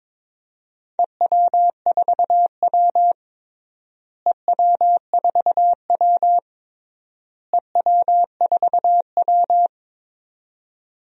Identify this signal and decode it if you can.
Morse code, heard as a keyed tone, about 22 wpm, 700 Hz. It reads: EW4W EW4W EW4W
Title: DXCC-100 Call Signs - 4 Characters - Repeated 3x - 22wpm